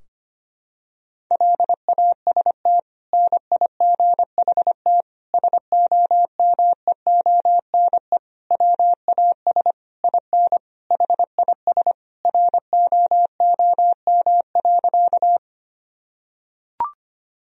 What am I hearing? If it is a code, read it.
Morse code, heard as a keyed tone, about 25 wpm, 700 Hz. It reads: LAST NIGHT SOMEONE WAS IN HIS ROOM.
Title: Sentences from Top 1000 Words - Review Part 4 - 25wpm